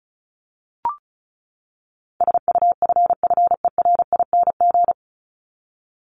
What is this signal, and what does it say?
Morse code, heard as a keyed tone, about 34 wpm, 700 Hz. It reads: SUFFERING
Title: ING Suffix - Top 100 Words with Base Words 35wpm